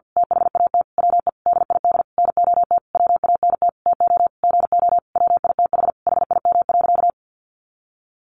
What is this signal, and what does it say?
Morse code, heard as a keyed tone, about 50 wpm, 700 Hz. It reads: THAT WE DID NOT WANT TO GO WITH HIM.
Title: Sentences from Top 1300 Words 50wpm